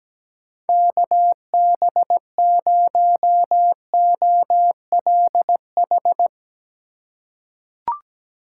Morse code, heard as a keyed tone, about 17 wpm, 700 Hz. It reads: KB0OLH
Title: USA Call Signs - 6 Characters - 4x Character Spacing - 17wpm